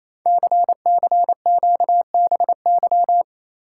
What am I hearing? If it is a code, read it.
Morse code, heard as a keyed tone, about 29 wpm, 700 Hz. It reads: CCQBY
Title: Code Groups - 5 Letters - 28wpm